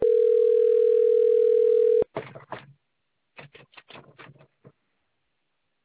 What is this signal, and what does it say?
Ringback tone (North America), 440 + 480 Hz.